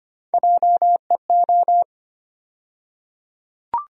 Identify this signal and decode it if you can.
Morse code, heard as a keyed tone, about 25 wpm, 700 Hz. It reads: JEO
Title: USA Call Sign Postfixes - 3 Letters 25wpm